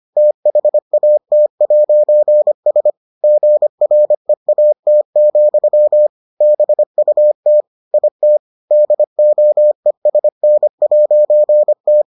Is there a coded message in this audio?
Yes: Morse code from 0.2 to 12.0 s, heard as a keyed tone, about 25 wpm, 600 Hz. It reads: THAT'S GREAT, BUT IT DOESN'T